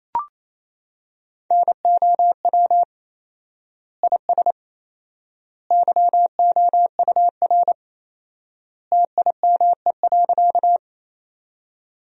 Morse code, heard as a keyed tone, about 28 wpm, 700 Hz. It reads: NOW IS YOUR TIME.
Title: 4x Word-Spacing - Sentences from Top 100 Words - 28wpm